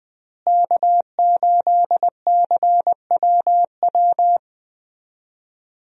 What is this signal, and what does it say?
Morse code, heard as a keyed tone, about 20 wpm, 700 Hz. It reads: K8CWW
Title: USA Call Signs - 5 Characters - Repeated 2x - 20wpm